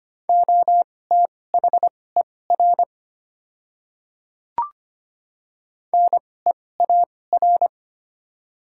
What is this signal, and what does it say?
Morse code, heard as a keyed tone, about 25 wpm, spaced as for 18 wpm, 700 Hz. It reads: OTHER NEAR
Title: 2x Character Spacing - Top 300 Words - 25wpm